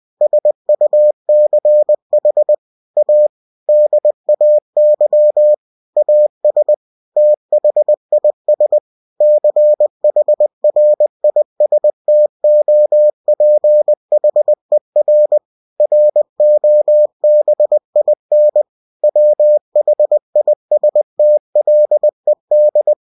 Morse code, heard as a keyed tone, about 20 wpm, 600 Hz. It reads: SUCH A DAY AS THIS CHRISTOPHER ROBIN WHISTLED
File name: cw-books / winnie-the-pooh / winnie-20wpm-page-0102